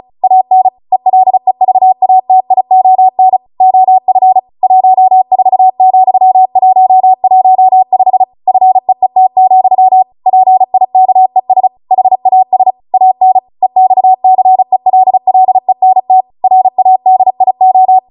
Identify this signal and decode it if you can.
Morse code, heard as a keyed tone, about 36 wpm, 750 Hz. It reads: AN ELEVATION OF 14,115 FEET, PIKES HAS AN EXCELLENT RADIO
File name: Morse / 35WPM